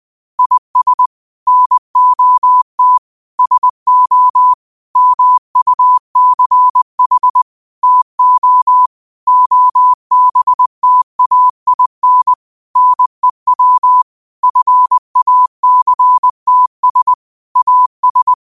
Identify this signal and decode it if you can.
Morse code, heard as a keyed tone, about 20 wpm, 1000 Hz. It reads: IS NOT SO MUCH TO OBTAIN NEW FACTS AS
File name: qotdfiles / 20wpm